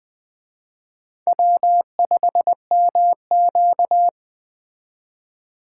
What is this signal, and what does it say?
Morse code, heard as a keyed tone, about 21 wpm, 700 Hz. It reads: W5MQ
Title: Speed Racing - USA Call Signs 17wpm